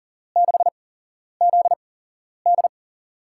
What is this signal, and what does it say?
Morse code, heard as a keyed tone, about 40 wpm, spaced as for 11 wpm, 700 Hz. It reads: 6ZD
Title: USA Call Signs - 5 Characters - 8x Character Spacing - 40wpm